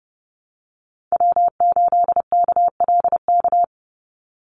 Morse code, heard as a keyed tone, about 30 wpm, 700 Hz. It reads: W8KLK
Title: USA Call Signs 30wpm